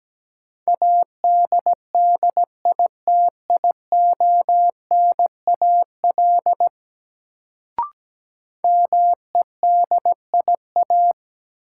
Morse code, heard as a keyed tone, about 17 wpm, 700 Hz. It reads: ADDITIONAL MEDIA
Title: Sentences from Top 1000 Words - Review Part 4 - New Word Drill - 17wpm